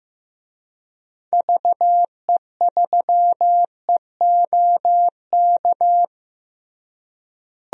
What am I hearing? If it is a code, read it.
Morse code, heard as a keyed tone, about 15 wpm, 700 Hz. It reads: VE3EOK
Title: DXCC-100 Call Signs - 6 Characters - Repeated 3x - 15wpm